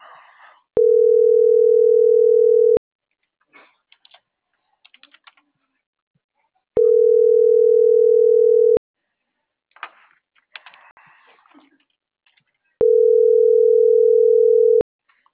Ringback tone (North America), 440 + 480 Hz.